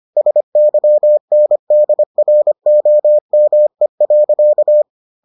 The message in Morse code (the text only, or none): SYNDROME.